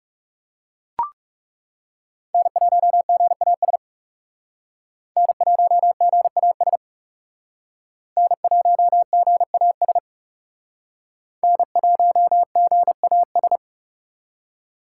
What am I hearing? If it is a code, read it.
Morse code, heard as a keyed tone, about 37 wpm, 700 Hz. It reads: N1GAS N1GAS N1GAS N1GAS
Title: Speed Racing - USA Call Signs 30wpm